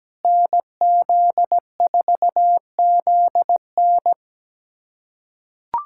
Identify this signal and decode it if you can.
Morse code, heard as a keyed tone, about 17 wpm, 700 Hz. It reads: NZ4ZN
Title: USA Callsign DE Callsign - 5 Characters - 17wpm